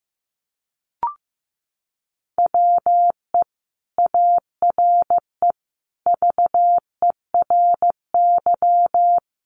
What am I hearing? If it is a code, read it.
Morse code, heard as a keyed tone, about 14 wpm, 700 Hz. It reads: WE ARE VERY
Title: Sentences from Top 800 Words 15wpm